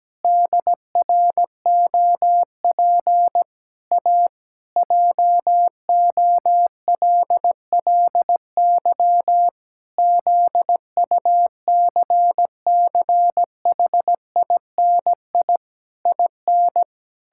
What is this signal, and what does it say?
Morse code, heard as a keyed tone, about 17 wpm, 700 Hz. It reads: DROP A JOLLY ZUCCHINI IN
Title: Pangram Challenge v2 - 17wpm